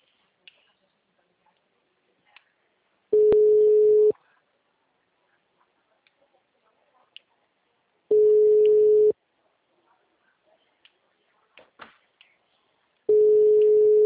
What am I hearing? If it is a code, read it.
Ringback tone (Europe, 425 Hz).